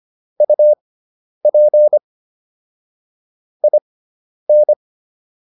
Morse code, heard as a keyed tone, about 25 wpm, spaced as for 10 wpm, 600 Hz. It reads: UP IN